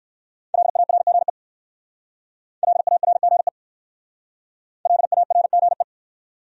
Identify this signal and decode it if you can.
Morse code, heard as a keyed tone, about 64 wpm, 700 Hz. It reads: LARGE LARGE LARGE